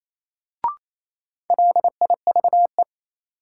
Morse code, heard as a keyed tone, about 28 wpm, 700 Hz. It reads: LIVE
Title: Sentences from Top 200 Words - New Word Drill - 28wpm